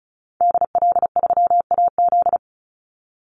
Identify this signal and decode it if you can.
Morse code, heard as a keyed tone, about 34 wpm, 700 Hz. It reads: DL3AZ
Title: Call Signs - 35wpm